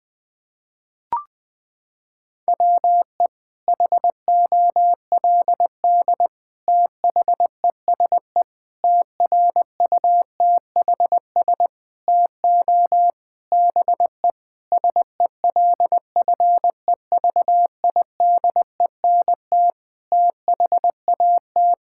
Morse code, heard as a keyed tone, about 21 wpm, 700 Hz. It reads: WE HOLD THESE TRUTHS TO BE SELFEVIDENT THAT
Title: Sets of 9 Words - 20wpm